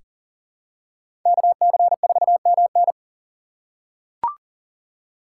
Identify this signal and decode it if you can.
Morse code, heard as a keyed tone, about 38 wpm, 700 Hz. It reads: KC4MN
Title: USA Call Signs - 5 Characters - 4x Character Spacing - 40wpm